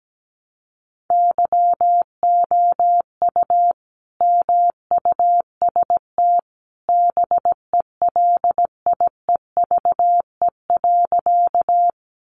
Morse code, heard as a keyed tone, about 17 wpm, 700 Hz. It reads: YOU MUST BELIEVE.
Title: Sentences from Top 1000 Words 17wpm